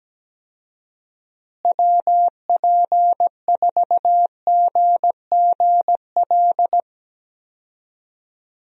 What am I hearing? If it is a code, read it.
Morse code, heard as a keyed tone, about 17 wpm, 700 Hz. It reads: WP4GGL
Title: USA Call Signs - 6 Characters - Repeated 3x - 17wpm